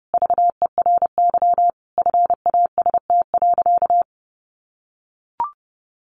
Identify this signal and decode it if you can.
Morse code, heard as a keyed tone, about 30 wpm, 700 Hz. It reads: VERY FAST.
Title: Sentences from Top 1100 Words 30wpm